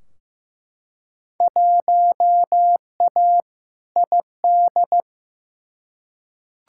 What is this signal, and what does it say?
Morse code, heard as a keyed tone, about 15 wpm, 700 Hz. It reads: 1A ID
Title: Field Day Exchanges - 15wpm